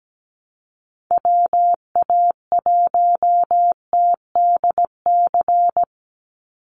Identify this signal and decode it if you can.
Morse code, heard as a keyed tone, about 17 wpm, 700 Hz. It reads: WA1TDC
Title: USA Call Signs 17wpm